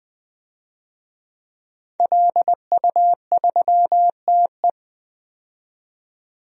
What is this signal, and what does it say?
Morse code, heard as a keyed tone, about 20 wpm, 700 Hz. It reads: LU3TE
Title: DXCC-100 Call Signs - 5 Characters - 20wpm